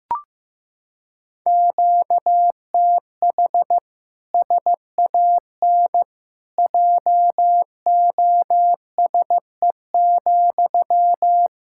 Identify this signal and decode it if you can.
Morse code, heard as a keyed tone, about 15 wpm, 700 Hz. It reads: QTH SAN JOSE,
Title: Practice QSO - Exchange 1 - Beginning - 15wpm